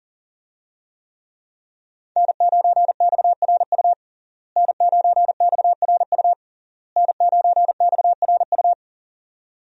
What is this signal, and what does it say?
Morse code, heard as a keyed tone, about 37 wpm, 700 Hz. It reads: N9XRU N9XRU N9XRU
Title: USA Call Signs - 5 Characters - Repeated 3x - 40wpm